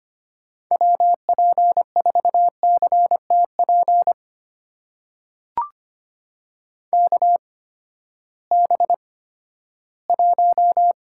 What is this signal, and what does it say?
Morse code, heard as a keyed tone, about 25 wpm, spaced as for 14 wpm, 700 Hz. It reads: WP4CTP KB1
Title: USA Call Signs - 6 Characters - 8x Character Spacing - 25wpm